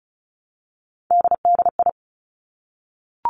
Morse code, heard as a keyed tone, about 36 wpm, 700 Hz. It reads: DDI
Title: Three Challenging Letter-Number Combinations 35wpm